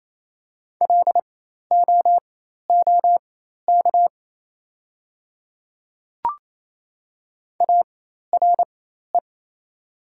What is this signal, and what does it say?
Morse code, heard as a keyed tone, about 28 wpm, spaced as for 13 wpm, 690 Hz. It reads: LOOK ARE
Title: 4x Character Spacing - Top 100 Words - 28wpm